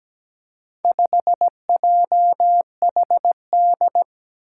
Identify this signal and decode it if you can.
Morse code, heard as a keyed tone, about 17 wpm, 700 Hz. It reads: 5JHD
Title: Four Challenging Letter-Number Combinations 17wpm